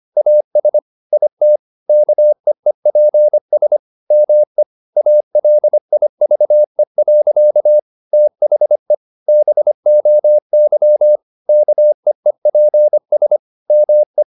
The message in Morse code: AS IT KEEPS ME ALIVE. THE BOY KEEPS ME

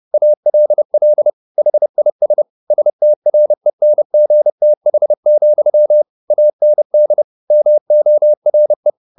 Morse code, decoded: ALL HIS STRENGTH, AND MORE

